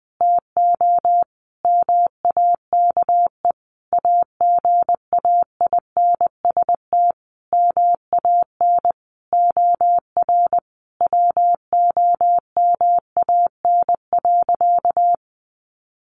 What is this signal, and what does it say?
Morse code, heard as a keyed tone, about 20 wpm, 700 Hz. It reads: TO MAKE AGAINST MAN OR WOMAN.
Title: Sentences from Top 1400 Words 20wpm